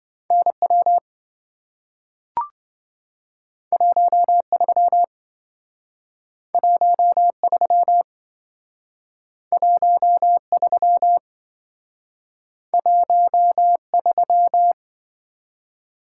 Morse code, heard as a keyed tone, about 27 wpm, 700 Hz. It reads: NW 13 13 13 13